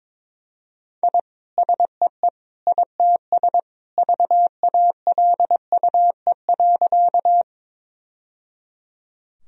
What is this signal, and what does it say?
Morse code, heard as a keyed tone, about 22 wpm, 700 Hz. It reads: I SEE ITS VALUE.
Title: Sentences from Top 1000 Words - Review Part 3 - 22wpm